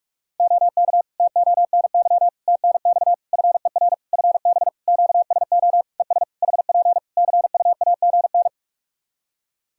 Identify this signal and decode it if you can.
Morse code, heard as a keyed tone, about 41 wpm, 700 Hz. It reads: OK TONY TNX FER FB QSO ES HP CUAGN
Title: Practice QSO - Exchange 4 - Ending - 45wpm